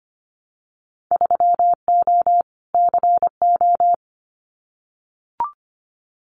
Morse code, heard as a keyed tone, about 25 wpm, 700 Hz. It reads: 3O CO
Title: Winter Field Day Exchanges Encore 25wpm